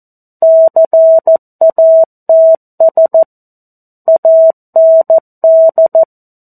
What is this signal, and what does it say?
Morse code, heard as a keyed tone, about 14 wpm, 650 Hz. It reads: CATS AND
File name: 02-Chapter-02 14 WPM